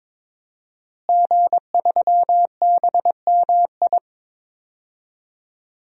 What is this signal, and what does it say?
Morse code, heard as a keyed tone, about 22 wpm, 700 Hz. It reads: G3BMI
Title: DXCC-100 Call Signs - 5 Characters - Repeated 2x - 22wpm